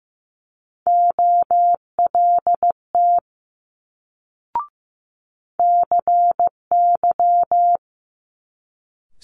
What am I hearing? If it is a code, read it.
Morse code, heard as a keyed tone, about 15 wpm, 700 Hz. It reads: OLT CY
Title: Multi-letter Phonemes Quick Fire 15wpm